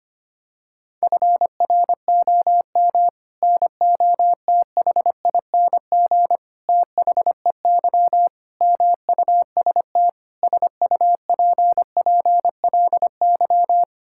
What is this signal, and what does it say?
Morse code, heard as a keyed tone, about 25 wpm, 700 Hz. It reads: FROM NOTHING THEY MUST SUPPLY
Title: Sentences from Top 1000 Words - Review Part 5 - 25wpm